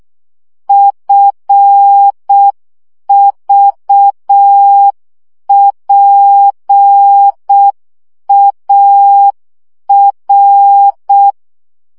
Morse code, heard as a keyed tone, about 6 wpm, 800 Hz. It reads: FVPAR
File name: morze vizsga 2011-1 5